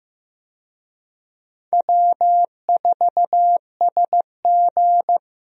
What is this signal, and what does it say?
Morse code, heard as a keyed tone, about 15 wpm, 700 Hz. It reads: W4SG
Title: USA Call Signs - 4 Characters - Repeated 2x - 15wpm